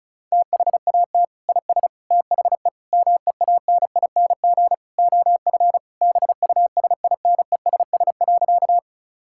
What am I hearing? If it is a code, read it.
Morse code, heard as a keyed tone, about 34 wpm, 700 Hz. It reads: THAT IS THE MEANING OF BUSINESS.
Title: Sentences from Top 1000 Words - Review Part 6 - 35wpm